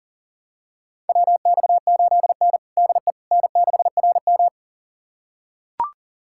Morse code, heard as a keyed tone, about 40 wpm, 690 Hz. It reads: WX8N DE N6RM
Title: USA Callsign DE Callsign - 4 Characters - 40wpm